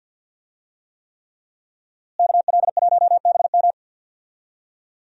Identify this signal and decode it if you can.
Morse code, heard as a keyed tone, about 50 wpm, 700 Hz. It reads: KL1BM
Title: USA Call Signs - 5 Characters - Repeated 3x - 50wpm